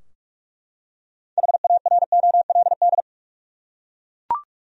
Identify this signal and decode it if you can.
Morse code, heard as a keyed tone, about 42 wpm, 700 Hz. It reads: HAROLD
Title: USA Common Names - 45wpm